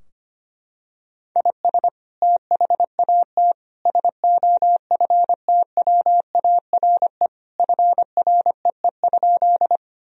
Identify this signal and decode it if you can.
Morse code, heard as a keyed tone, about 25 wpm, 700 Hz. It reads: IS THAT SOFTWARE FREE?
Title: Sentences from Top 1000 Words - Review Part 3 - 25wpm